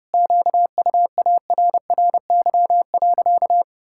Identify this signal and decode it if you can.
Morse code, heard as a keyed tone, about 30 wpm, 700 Hz. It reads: QUARRY.
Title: Pangram Challenge v2 - 30wpm